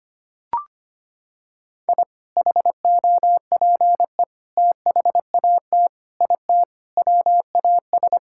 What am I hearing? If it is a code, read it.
Morse code, heard as a keyed tone, about 25 wpm, 700 Hz. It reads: I HOPE THAT IT WAS
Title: Sentences from Top 1000 Words - Review Part 5 - 25wpm